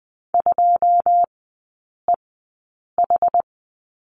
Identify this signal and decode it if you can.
Morse code, heard as a keyed tone, about 20 wpm, spaced as for 8 wpm, 700 Hz. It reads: 2EH
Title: 093 Lesson 16e - Letter c - Three Characters 20wpm